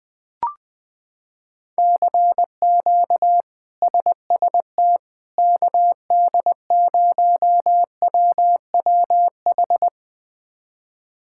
Morse code, heard as a keyed tone, about 21 wpm, 700 Hz. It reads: CQ SST KD0WWH
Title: K1USN SST QSO Practice - 20wpm